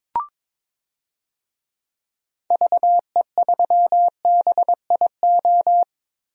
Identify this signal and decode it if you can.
Morse code, heard as a keyed tone, about 22 wpm, 700 Hz. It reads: VE3BIO
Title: DXCC-100 Call Signs - 6 Characters - 22wpm